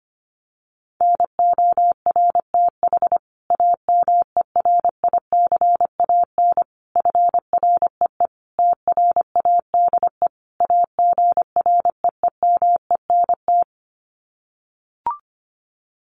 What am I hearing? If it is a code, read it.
Morse code, heard as a keyed tone, about 25 wpm, 700 Hz. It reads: NORTH AMERICAN FREE TRADE AGREEMENT
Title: Sets of 5 Words Encore 25wpm